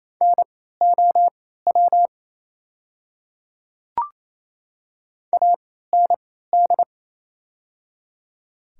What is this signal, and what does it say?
Morse code, heard as a keyed tone, about 28 wpm, spaced as for 16 wpm, 700 Hz. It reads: NOW AND
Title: 3x Character Spacing - Top 100 Words - 28wpm